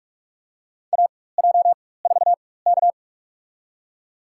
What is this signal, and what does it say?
Morse code, heard as a keyed tone, about 46 wpm, spaced as for 21 wpm, 700 Hz. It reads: AJ4K
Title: USA Call Signs - 4 Characters - 4x Character Spacing - 45wpm